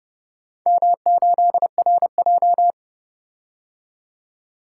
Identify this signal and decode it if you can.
Morse code, heard as a keyed tone, about 31 wpm, 700 Hz. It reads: M8RJ